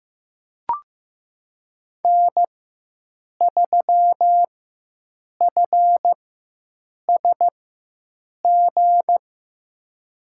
Morse code, heard as a keyed tone, about 16 wpm, spaced as for 7 wpm, 700 Hz. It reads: N3FSG
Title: USA Call Signs - 4x Character Spacing - 15wpm